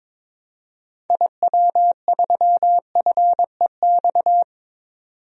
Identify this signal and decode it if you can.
Morse code, heard as a keyed tone, about 22 wpm, 700 Hz. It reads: IW3FEX
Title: DXCC-100 Call Signs - 6 Characters - 22wpm